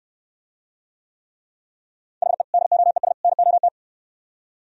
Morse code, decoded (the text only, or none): HE DOES NOT